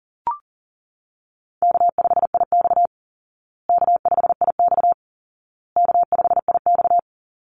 Morse code, heard as a keyed tone, about 37 wpm, 700 Hz. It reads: K5IX K5IX K5IX
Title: Call Signs Repeated 2x - 40wpm